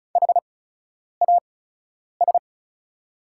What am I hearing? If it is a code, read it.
Morse code, heard as a keyed tone, about 36 wpm, spaced as for 10 wpm, 690 Hz. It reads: HAS
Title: 8x Character Spacing - Top 200 Words - 35wpm